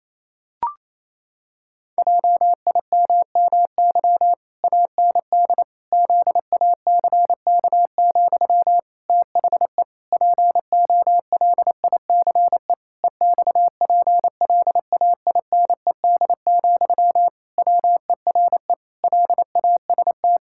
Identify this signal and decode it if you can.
Morse code, heard as a keyed tone, about 28 wpm, 700 Hz. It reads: JIMMY AND ZACK, THE POLICE EXPLAINED, WERE LAST
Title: Pangram Challenge v2 - 28wpm